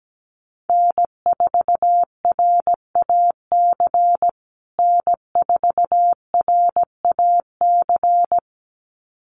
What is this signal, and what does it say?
Morse code, heard as a keyed tone, about 17 wpm, 700 Hz. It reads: N4RAC N4RAC
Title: USA Call Signs Repeated 2x 17wpm